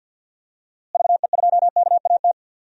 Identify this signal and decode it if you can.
Morse code, heard as a keyed tone, about 47 wpm, 700 Hz. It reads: VE2KAT
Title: DXCC-100 Call Signs - 6 Characters - Repeated 2x - 50wpm